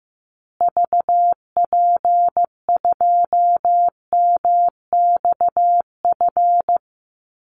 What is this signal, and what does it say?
Morse code, heard as a keyed tone, about 15 wpm, 700 Hz. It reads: VP2MXF
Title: Call Signs Repeated 3x - 15wpm